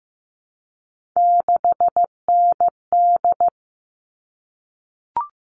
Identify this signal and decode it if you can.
Morse code, heard as a keyed tone, about 15 wpm, 700 Hz. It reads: 6ND
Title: All Cut Numbers - 3-Digit 15wpm